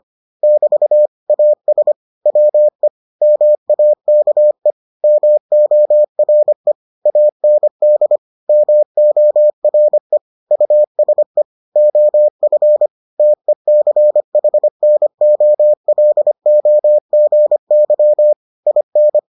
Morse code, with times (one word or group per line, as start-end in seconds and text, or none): 0.4-1.1 <BT>
1.3-1.9 AS
2.3-2.9 WE
3.2-4.7 MAKE
5.0-6.7 MORE
7.1-8.2 AND
8.5-10.2 MORE
10.5-11.4 USE
11.8-12.9 OF
13.2-18.3 TECHNOLOGY
18.7-19.2 IN